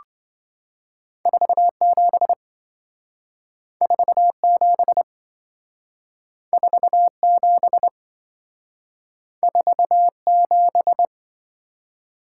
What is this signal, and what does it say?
Morse code, heard as a keyed tone, about 27 wpm, 700 Hz. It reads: 47 47 47 47